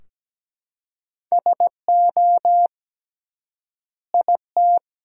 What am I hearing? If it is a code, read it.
Morse code, heard as a keyed tone, about 17 wpm, 700 Hz. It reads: SO IT
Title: 3x Word-Spacing - Sentences from Top 100 Words - 17wpm